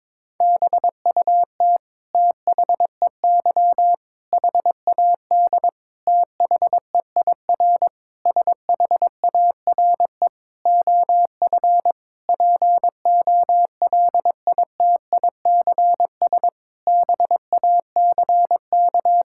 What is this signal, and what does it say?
Morse code, heard as a keyed tone, about 22 wpm, 700 Hz. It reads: BUT THEY HAD THEIR SHARE OF POLITICS BACK
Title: Sentences from Top 1000 Words - Review Part 6 - 22wpm